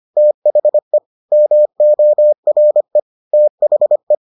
Morse code, heard as a keyed tone, about 25 wpm, 600 Hz. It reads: THE MORE THE